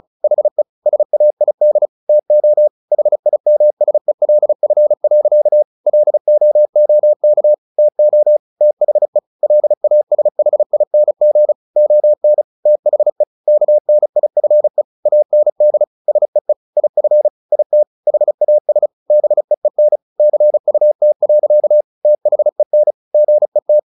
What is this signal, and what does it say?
Morse code, heard as a keyed tone, about 34 wpm, 600 Hz. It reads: HE SAID TO HIMSELF. LOOK TO THE LASHING ON THE KNIFE AND SEE IF IT HAS BEEN CUT. THEN GET
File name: the old man and the sea 2 35wpm